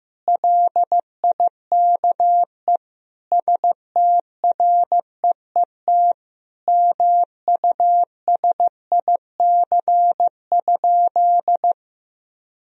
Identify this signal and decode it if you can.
Morse code, heard as a keyed tone, about 15 wpm, 700 Hz. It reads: LIKE STREET MUSIC?
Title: Sentences from Top 1000 Words - Review Part 3 - 15wpm